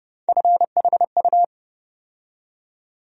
Morse code, heard as a keyed tone, about 30 wpm, 700 Hz. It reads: FHU
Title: USA Call Sign Postfixes - 3 Letters 30wpm